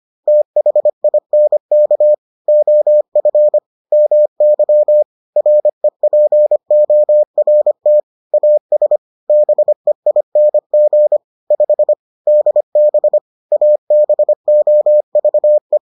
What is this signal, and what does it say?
Morse code, heard as a keyed tone, about 25 wpm, 600 Hz. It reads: THINK OF MY REPORT AS BEING 5 DB ABOVE